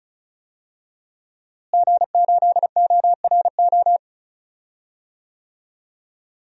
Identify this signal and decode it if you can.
Morse code, heard as a keyed tone, about 36 wpm, 700 Hz. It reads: G8ORO
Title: DXCC-100 Call Signs - 5 Characters - Repeated 3x - 35wpm